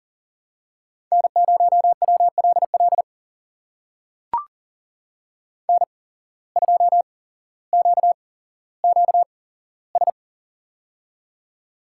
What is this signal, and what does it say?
Morse code, heard as a keyed tone, about 40 wpm, spaced as for 17 wpm, 700 Hz. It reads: N0WLL N2QQS
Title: USA Call Signs - 5 Characters - 8x Character Spacing - 40wpm